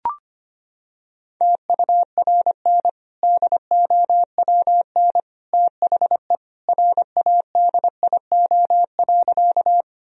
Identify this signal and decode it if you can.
Morse code, heard as a keyed tone, about 25 wpm, 700 Hz. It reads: TURN DOWN THE RADIO.
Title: Sentences from Top 1000 Words - Review Part 3 - 25wpm